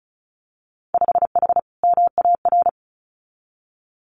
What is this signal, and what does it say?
Morse code, heard as a keyed tone, about 34 wpm, 700 Hz. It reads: 5H MAR
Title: Winter Field Day Exchanges Encore 35wpm